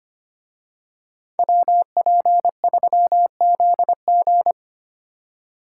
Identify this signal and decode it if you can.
Morse code, heard as a keyed tone, about 28 wpm, 700 Hz. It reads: WP3ZG
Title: Speed Racing - USA Call Signs 25wpm